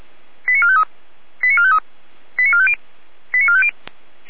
Five-tone selective call (ZVEI), heard as quick runs of five tones, tones of about 80 ms. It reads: 89453 89453 89450 89450
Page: ZVEI